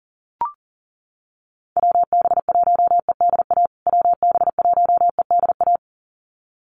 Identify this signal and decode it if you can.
Morse code, heard as a keyed tone, about 40 wpm, 700 Hz. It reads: WB1EDA WB1EDA
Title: USA Call Signs Repeated 2x 40wpm